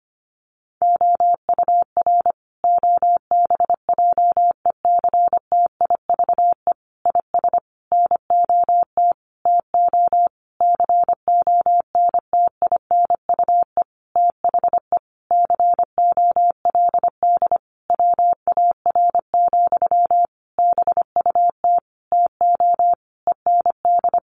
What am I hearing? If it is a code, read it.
Morse code, heard as a keyed tone, about 25 wpm, 700 Hz. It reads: OUR OBJECTIVE IS NOT TO CONTINUE THE COLD WAR, BUT TO END
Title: Sentences from Top 1200 Words 25wpm